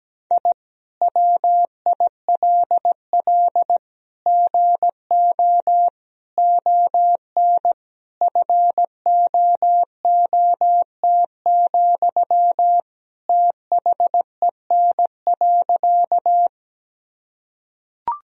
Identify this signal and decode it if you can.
Morse code, heard as a keyed tone, about 17 wpm, 700 Hz. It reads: I WILL GO ON FOOT, THEN.
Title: Sentences from Top 1000 Words - Review Part 3 - 17wpm